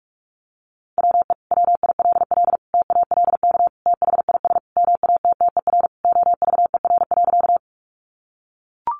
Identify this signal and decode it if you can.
Morse code, heard as a keyed tone, about 44 wpm, 700 Hz. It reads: WE WILL TALK THIS MATTER OVER.
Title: Sentences from Top 700 Words 45wpm